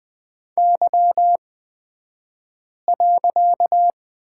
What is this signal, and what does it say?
Morse code, heard as a keyed tone, about 20 wpm, spaced as for 5 wpm, 700 Hz. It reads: Y.
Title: 254 Lesson 41d - Period - Two Characters 20wpm